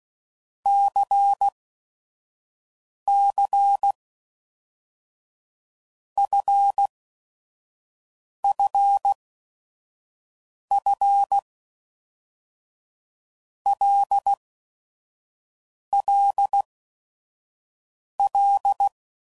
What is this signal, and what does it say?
Morse code, heard as a keyed tone, about 16 wpm, 800 Hz. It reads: C C F F F L L L
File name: LFCYQ@5 f15